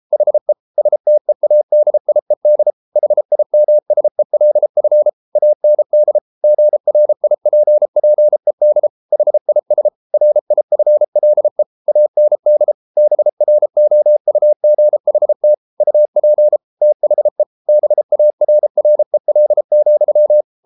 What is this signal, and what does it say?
Morse code, heard as a keyed tone, about 33 wpm, 600 Hz. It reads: HE STEADIED HIMSELF AND GRIPPED HIS RIFLE AND BROUGHT UP THE BARREL,